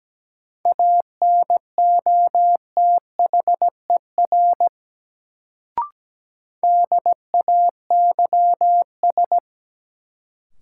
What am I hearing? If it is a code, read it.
Morse code, heard as a keyed tone, about 17 wpm, 700 Hz. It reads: ANOTHER DAYS
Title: Sets of 5 Words Encore - New Word Drill - 17wpm